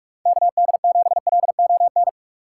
Morse code, heard as a keyed tone, about 46 wpm, 700 Hz. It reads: KD7LON
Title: USA Call Signs - 6 Characters - 2x Character Spacing - 45wpm